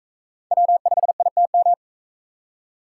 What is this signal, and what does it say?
Morse code, heard as a keyed tone, about 40 wpm, 700 Hz. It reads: W5ITM